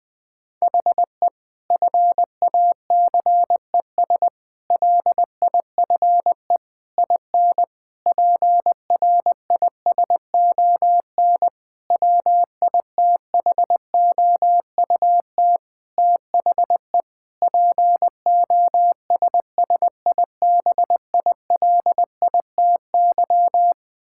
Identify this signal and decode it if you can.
Morse code, heard as a keyed tone, about 20 wpm, 700 Hz. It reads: HE FACES LIFE IN PRISON WITHOUT THE POSSIBILITY
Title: Sentences with 8-Word Phrases - 20wpm